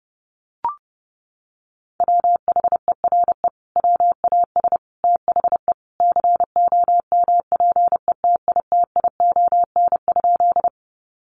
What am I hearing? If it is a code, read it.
Morse code, heard as a keyed tone, about 30 wpm, 700 Hz. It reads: WHERE WAS THE COMPETITION?
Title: Sentences from Top 1000 Words 30wpm